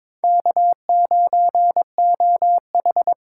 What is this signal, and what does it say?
Morse code, heard as a keyed tone, about 21 wpm, 700 Hz. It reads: K9OH
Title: USA Call Signs - 4 Characters - 3x Character Spacing - 22wpm